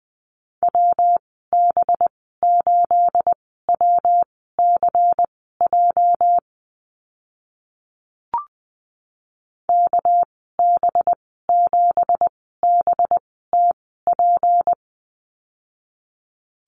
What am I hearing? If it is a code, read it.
Morse code, heard as a keyed tone, about 20 wpm, spaced as for 14 wpm, 700 Hz. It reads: WB8WCJ KB7BTP
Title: 219 Lesson 35g - Number 0 - Call Signs 20wpm